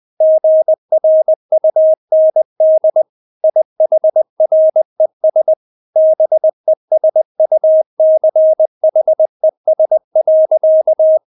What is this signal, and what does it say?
Morse code, heard as a keyed tone, about 20 wpm, 630 Hz. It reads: GRUND IHRES BESUCHES.
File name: Witz1 20wpm